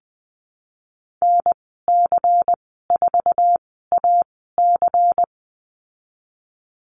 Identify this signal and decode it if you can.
Morse code, heard as a keyed tone, about 20 wpm, spaced as for 15 wpm, 700 Hz. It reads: NC4AC
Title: 122 Lesson 20g - Number 3 - Call Signs 20wpm